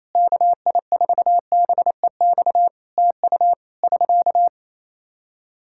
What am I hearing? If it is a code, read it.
Morse code, heard as a keyed tone, about 28 wpm, 700 Hz. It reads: KI4BEX TU <SK>
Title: K1USN SST QSO Practice 28wpm